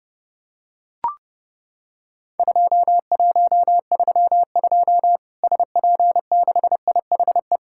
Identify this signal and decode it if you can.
Morse code, heard as a keyed tone, about 30 wpm, 700 Hz. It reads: 2132 SP6IHE
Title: Worked All Europe DX Contest - QTC Exchange - 30wpm